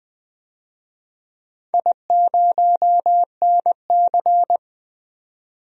Morse code, heard as a keyed tone, about 20 wpm, 700 Hz. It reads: I0NC
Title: DXCC-100 Call Signs - 4 Characters - 20wpm